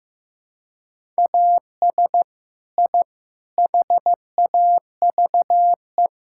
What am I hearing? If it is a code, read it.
Morse code, heard as a keyed tone, about 15 wpm, 700 Hz. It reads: AS I HAVE
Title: Sentences from Top 1000 Words - Review Part 6 - 15wpm